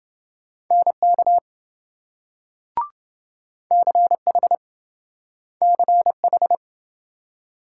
Morse code, heard as a keyed tone, about 28 wpm, 700 Hz. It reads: NK CH CH